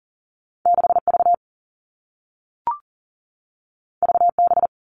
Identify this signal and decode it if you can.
Morse code, heard as a keyed tone, about 40 wpm, 700 Hz. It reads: 6V VB
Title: Two Challenging Letter-Number Combinations 40wpm